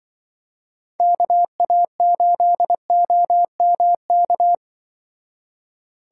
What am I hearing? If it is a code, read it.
Morse code, heard as a keyed tone, about 24 wpm, 700 Hz. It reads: KA8OMK
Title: Speed Racing - USA 2x3 Call Signs 20wpm